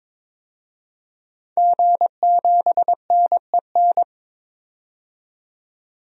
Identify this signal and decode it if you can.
Morse code, heard as a keyed tone, about 22 wpm, 700 Hz. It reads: G7NEN
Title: DXCC-100 Call Signs - 5 Characters - 22wpm